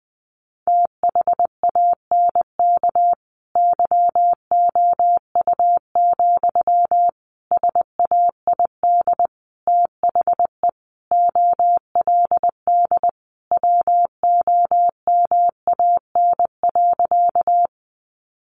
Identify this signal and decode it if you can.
Morse code, heard as a keyed tone, about 19 wpm, 700 Hz. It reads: THANK YOU, SAID THE OLD WOMAN.
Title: Sentences from Top 1100 Words 20wpm